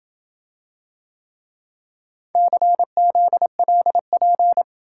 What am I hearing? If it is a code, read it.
Morse code, heard as a keyed tone, about 27 wpm, 700 Hz. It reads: CZLP